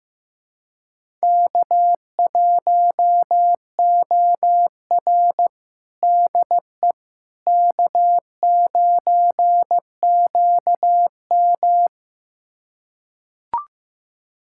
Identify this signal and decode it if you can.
Morse code, heard as a keyed tone, about 15 wpm, 700 Hz. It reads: K1OR DE K9QM
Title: USA Callsign DE Callsign - 4 Characters - 15wpm